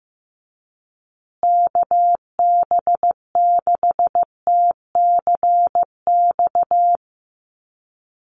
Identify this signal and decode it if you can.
Morse code, heard as a keyed tone, about 15 wpm, 700 Hz. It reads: KB6TCX
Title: USA Call Signs Repeated 2x 15wpm